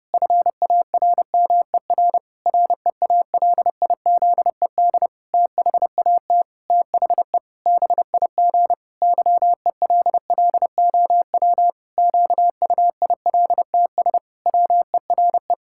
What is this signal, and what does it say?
Morse code, heard as a keyed tone, about 30 wpm, 700 Hz. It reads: FARMER REALIZED THAT THE BIG YELLOW QUILTS WERE
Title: Pangram Challenge v2 - 30wpm